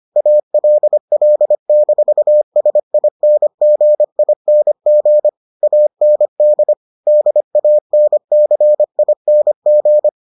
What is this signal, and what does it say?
Morse code, heard as a keyed tone, about 25 wpm, 600 Hz. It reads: ALL-SINGING AND DANCING